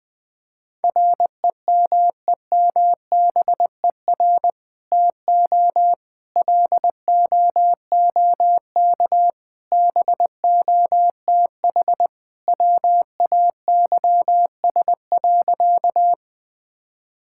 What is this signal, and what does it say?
Morse code, heard as a keyed tone, about 20 wpm, 700 Hz. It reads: REMEMBER TO LOOK BOTH WAYS.
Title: Sentences with 2-Word Phrases - Spoken Beginning - 20wpm